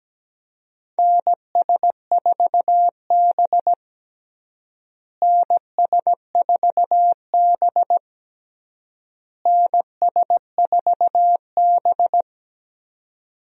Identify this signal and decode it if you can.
Morse code, heard as a keyed tone, about 17 wpm, 700 Hz. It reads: NS4B NS4B NS4B
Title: USA Call Signs - 4 Characters - Repeated 3x - 17wpm